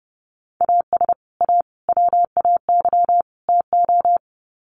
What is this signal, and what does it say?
Morse code, heard as a keyed tone, about 30 wpm, 700 Hz. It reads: AS A WAY TO
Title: Sets of 4 Words 30wpm